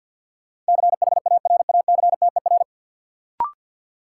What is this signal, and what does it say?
Morse code, heard as a keyed tone, about 52 wpm, 700 Hz. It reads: CHARACTER
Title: Sentences from Top 1200 Words - New Word Drill Alt - 50wpm